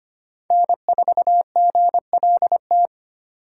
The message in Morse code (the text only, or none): N4GLT